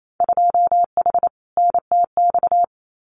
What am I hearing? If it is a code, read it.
Morse code, heard as a keyed tone, about 28 wpm, 700 Hz. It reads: 2H NTX
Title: Winter Field Day Exchanges Encore 28wpm